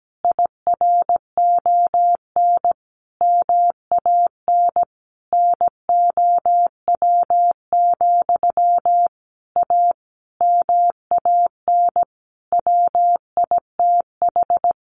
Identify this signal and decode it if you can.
Morse code, heard as a keyed tone, about 17 wpm, 700 Hz. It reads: IRON MAN NOW, A MAN WITH
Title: A Sound Of Thunder 17wpm